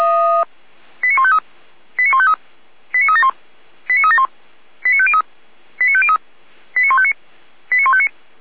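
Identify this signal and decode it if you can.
Five-tone selective call (ZVEI), heard as quick runs of five tones, tones of about 70 ms. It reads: 89153 89153 89371 89371 89503 89503 89159 89159
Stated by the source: ZVEI five-tone selective call received by SDR, from feuerwehr-orlamuende (fire brigade Orlamünde) archive.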